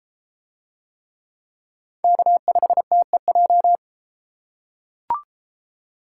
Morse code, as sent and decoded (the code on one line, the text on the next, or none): -.- ..... - . .---
K5TEJ